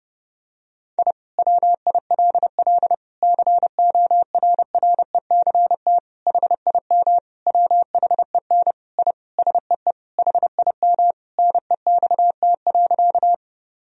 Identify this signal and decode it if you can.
Morse code, heard as a keyed tone, about 30 wpm, 700 Hz. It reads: I WILL CORRECT HIM WHEN I SEE HIM NEXT.
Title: Sentences from Top 1000 Words - Review Part 2 - 30wpm